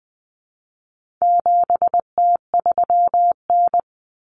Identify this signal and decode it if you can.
Morse code, heard as a keyed tone, about 20 wpm, 700 Hz. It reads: 7T3N
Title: Common Cut Numbers - 4-Digit 20wpm